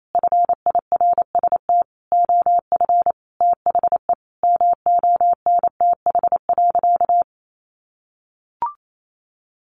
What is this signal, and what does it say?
Morse code, heard as a keyed tone, about 28 wpm, 700 Hz. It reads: FIRST OF THE MONTH.
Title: Sentences from Top 800 Words 28wpm